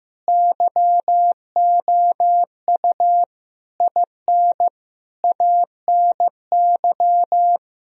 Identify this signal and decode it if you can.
Morse code, heard as a keyed tone, about 15 wpm, 700 Hz. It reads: YOU IN ANY
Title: Sentences from Top 1000 Words - Review Part 1 - 15wpm